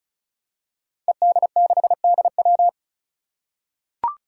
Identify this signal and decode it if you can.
Morse code, heard as a keyed tone, about 36 wpm, 700 Hz. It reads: ED6DW
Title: Five Challenging Letter-Number Combinations 35wpm